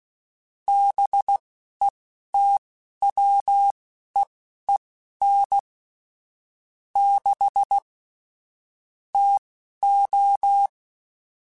Morse code, heard as a keyed tone, about 16 wpm, spaced as for 11 wpm, 800 Hz. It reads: BETWEEN 6 TO
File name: Weather Sets / Weather Sets-mp3s / Weather 4 24 @10